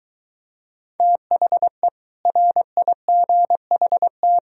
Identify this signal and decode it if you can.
Morse code, heard as a keyed tone, about 23 wpm, 700 Hz. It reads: THE RIGHT